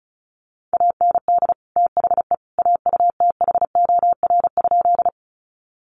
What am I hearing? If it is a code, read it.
Morse code, heard as a keyed tone, about 34 wpm, 700 Hz. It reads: AND THE AUTHOR?
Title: Sentences from Top 1200 Words 35wpm